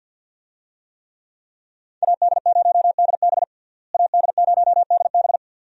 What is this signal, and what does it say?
Morse code, heard as a keyed tone, about 49 wpm, 700 Hz. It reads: AD0DB AD0DB
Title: USA Call Signs - 5 Characters - Repeated 2x - 50wpm